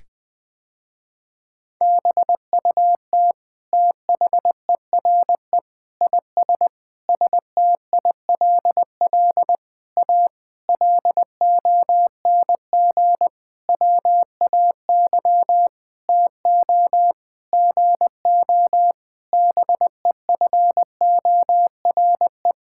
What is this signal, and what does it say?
Morse code, heard as a keyed tone, about 21 wpm, 700 Hz. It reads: BUT THERE IS STILL A LONG WAY TO GO BEFORE
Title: Sentences with 9-Word Phrases - Spoken Beginning - 20wpm